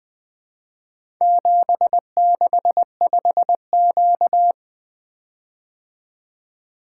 Morse code, heard as a keyed tone, about 23 wpm, 700 Hz. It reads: 765Q
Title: Speed-Racing - Four Letter-Number Combinations - 20wpm